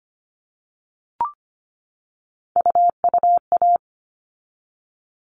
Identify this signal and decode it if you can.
Morse code, heard as a keyed tone, about 25 wpm, 700 Hz. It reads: UUA
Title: All Cut Numbers - 3-Digit 25wpm